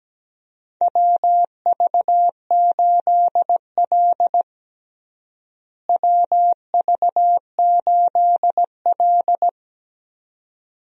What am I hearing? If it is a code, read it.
Morse code, heard as a keyed tone, about 17 wpm, 700 Hz. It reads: WV8L WV8L
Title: USA Call Signs - 4 Characters - Repeated 2x - 17wpm